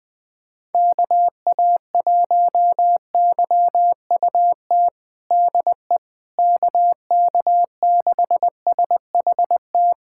Morse code, heard as a keyed tone, about 20 wpm, 700 Hz. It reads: KA1YUT DE KK6SHT
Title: USA Callsign DE Callsign 20wpm